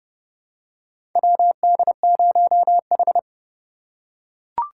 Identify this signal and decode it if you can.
Morse code, heard as a keyed tone, about 30 wpm, 700 Hz. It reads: WD0H
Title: USA Call Signs - 4 Characters - 3x Character Spacing - 30wpm